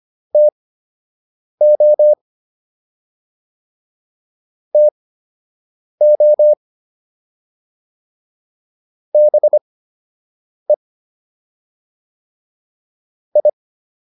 Morse code, heard as a keyed tone, about 25 wpm, spaced as for 7 wpm, 600 Hz. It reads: TO TO BE I